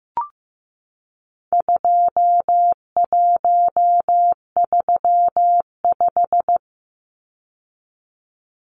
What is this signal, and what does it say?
Morse code, heard as a keyed tone, about 15 wpm, 700 Hz. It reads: 2135
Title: Numbers - 4-Digit 15wpm